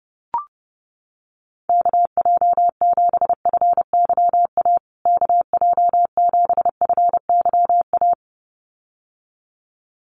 Morse code, heard as a keyed tone, about 30 wpm, 700 Hz. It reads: KJ7FYA KJ7FYA
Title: USA Call Signs Repeated 2x 30wpm